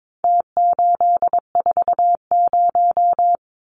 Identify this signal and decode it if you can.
Morse code, heard as a keyed tone, about 22 wpm, 700 Hz. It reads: T840
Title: Numbers - Contest Style - 4-Digit 22wpm